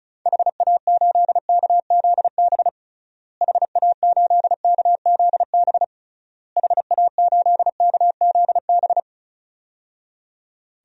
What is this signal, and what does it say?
Morse code, heard as a keyed tone, about 34 wpm, 700 Hz. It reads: HA8KZB HA8KZB HA8KZB
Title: DXCC-100 Call Signs - 6 Characters - Repeated 3x - 35wpm